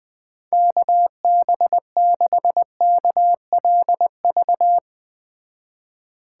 Morse code, heard as a keyed tone, about 20 wpm, 700 Hz. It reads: KB6KLV
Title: Speed Racing - USA Call Signs 17wpm